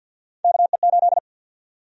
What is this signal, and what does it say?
Morse code, heard as a keyed tone, about 50 wpm, 700 Hz. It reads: KE8
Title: USA Call Sign Prefixes - 50wpm